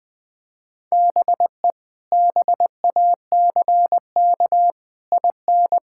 Morse code, heard as a keyed tone, about 20 wpm, 700 Hz. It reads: BE BACK IN
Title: Build A Phrase - 5-Word Phrases - 20wpm